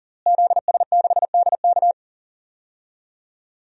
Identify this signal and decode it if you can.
Morse code, heard as a keyed tone, about 38 wpm, 700 Hz. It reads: ZS6DK
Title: DXCC-100 Call Signs - 5 Characters - Repeated 2x - 40wpm